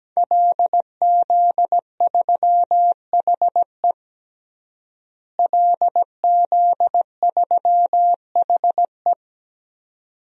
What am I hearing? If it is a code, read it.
Morse code, heard as a keyed tone, about 17 wpm, 700 Hz. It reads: LZ3HE LZ3HE
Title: DXCC-100 Call Signs - 5 Characters - Repeated 2x - 17wpm